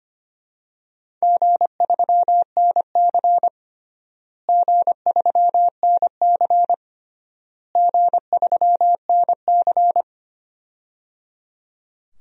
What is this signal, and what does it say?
Morse code, heard as a keyed tone, about 25 wpm, 700 Hz. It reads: G3NC G3NC G3NC
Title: DXCC-100 Call Signs - 4 Characters - Repeated 3x - 25wpm